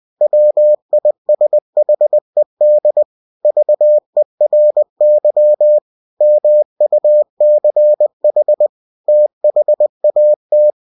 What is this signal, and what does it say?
Morse code, heard as a keyed tone, about 20 wpm, 600 Hz. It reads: WISHED VERY MUCH THAT
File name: cw-books / winnie-the-pooh / winnie-20wpm-page-0023